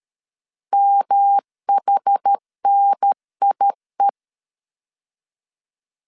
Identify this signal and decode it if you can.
Morse code, heard as a keyed tone, about 13 wpm, 800 Hz. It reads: MHNIE
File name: CW / DET / 1 3 1500C S B 12wpm 2828